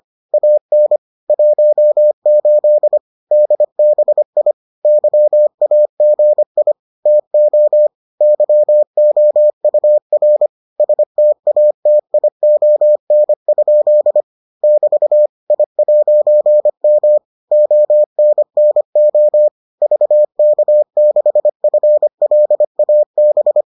Morse code, heard as a keyed tone, about 25 wpm, 600 Hz. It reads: AN 18 DBI YAGI TO YOUR STATION? <BT> I'M ONNO VK6FLAB